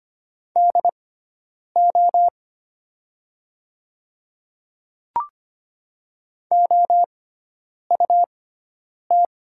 Morse code, heard as a keyed tone, about 25 wpm, spaced as for 9 wpm, 700 Hz. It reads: DO OUT
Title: 6x Character Spacing - Top 100 Words - 25wpm